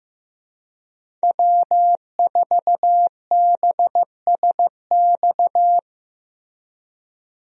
Morse code, heard as a keyed tone, about 15 wpm, 700 Hz. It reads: W4BSX
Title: USA Call Signs - 5 Characters - Repeated 2x - 15wpm